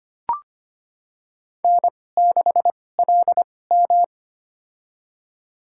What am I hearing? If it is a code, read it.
Morse code, heard as a keyed tone, about 25 wpm, spaced as for 18 wpm, 700 Hz. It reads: N6LM
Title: USA Call Signs - 4 Characters - 2x Character Spacing - 25wpm